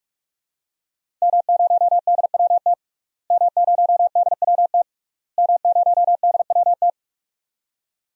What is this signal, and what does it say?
Morse code, heard as a keyed tone, about 41 wpm, 700 Hz. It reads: M0DWT M0DWT M0DWT
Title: DXCC-100 Call Signs - 5 Characters - Repeated 3x - 45wpm